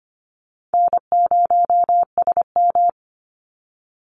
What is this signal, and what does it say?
Morse code, heard as a keyed tone, about 25 wpm, 700 Hz. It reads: N0SM
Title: Call Signs - 25wpm